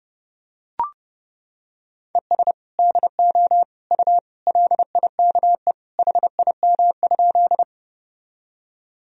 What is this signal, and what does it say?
Morse code, heard as a keyed tone, about 30 wpm, 700 Hz. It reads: ES DO U LIKE HIM?
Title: Sentences from Top 100 Words and Abbreviations 30wpm